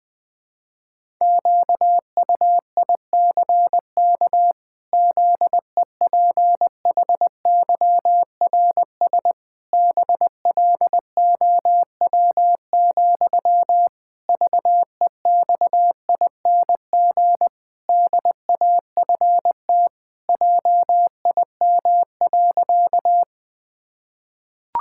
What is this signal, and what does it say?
Morse code, heard as a keyed tone, about 20 wpm, 700 Hz. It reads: QUICK ZEPHYRS BLOW, VEXING DAFT JIM.
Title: Pangram Challenge v2 - 20wpm